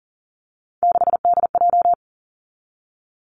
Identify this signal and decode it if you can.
Morse code, heard as a keyed tone, about 42 wpm, 690 Hz. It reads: 6DJ
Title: Three Challenging Letter-Number Combinations 40wpm